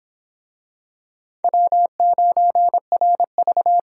Morse code, heard as a keyed tone, about 26 wpm, 700 Hz. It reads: W9RV